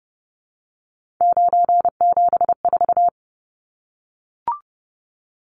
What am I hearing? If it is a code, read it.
Morse code, heard as a keyed tone, about 30 wpm, 700 Hz. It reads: 974
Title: Numbers - 3-Digit 30wpm